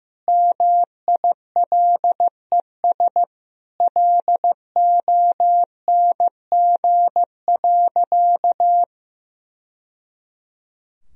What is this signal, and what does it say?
Morse code, heard as a keyed tone, about 16 wpm, 700 Hz. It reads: MILES LONG.
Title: Sentences from Top 1000 Words - Review Part 3 - 15wpm